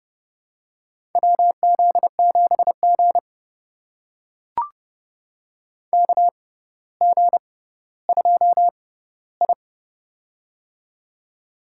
Morse code, heard as a keyed tone, about 32 wpm, spaced as for 16 wpm, 700 Hz. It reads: WZ7G KG2I
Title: USA Call Signs - 4 Characters - 6x Character Spacing - 30wpm